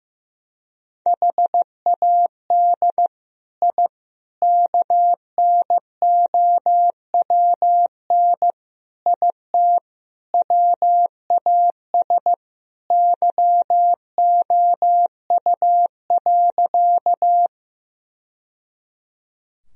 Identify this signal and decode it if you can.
Morse code, heard as a keyed tone, about 16 wpm, 700 Hz. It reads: HAD I KNOWN IT WAS YOU.
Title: Sentences from Top 1000 Words - Review Part 3 - 15wpm